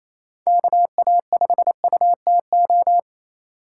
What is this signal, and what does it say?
Morse code, heard as a keyed tone, about 28 wpm, 700 Hz. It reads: KA5UTO
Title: USA Call Signs - 6 Characters - 4x Character Spacing - 28wpm